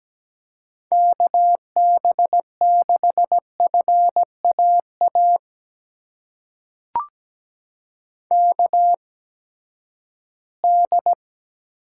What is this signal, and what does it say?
Morse code, heard as a keyed tone, about 17 wpm, spaced as for 12 wpm, 700 Hz. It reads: KB6FAA KD
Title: USA Call Signs - 6 Characters - 8x Character Spacing - 17wpm